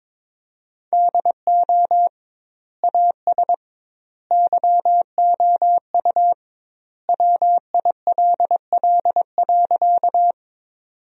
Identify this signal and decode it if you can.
Morse code, heard as a keyed tone, about 22 wpm, 700 Hz. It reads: DO AS YOU WILL.
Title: 2x Word-Spacing - Sentences from Top 100 Words - 22wpm